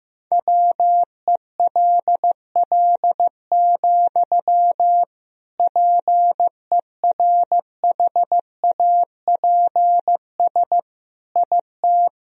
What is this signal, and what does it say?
Morse code, heard as a keyed tone, about 15 wpm, 700 Hz. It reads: WELL, PERHAPS IT
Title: Sentences from Top 500 Words and Abbreviations 15wpm